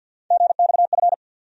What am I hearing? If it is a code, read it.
Morse code, heard as a keyed tone, about 51 wpm, 700 Hz. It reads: GXF
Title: USA Call Sign Postfixes - 3 Letters 50wpm